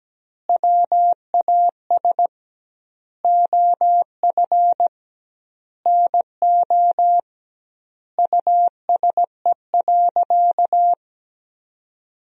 Morse code, heard as a keyed tone, about 17 wpm, 700 Hz. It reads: WAS OF NO USE.
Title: 2x Word-Spacing - Sentences from Top 100 Words - 17wpm